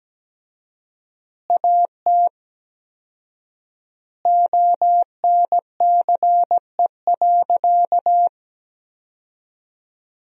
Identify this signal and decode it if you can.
Morse code, heard as a keyed tone, about 17 wpm, 700 Hz. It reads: AT ONCE.
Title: 4x Word-Spacing - Sentences from Top 400 Words - 17wpm